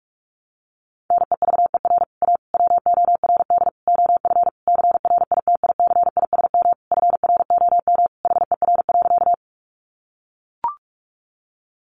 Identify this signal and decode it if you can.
Morse code, heard as a keyed tone, about 44 wpm, 700 Hz. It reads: NEVER A WORD OF CRITICISM FROM HER.
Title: Sentences from Top 1300 Words 45wpm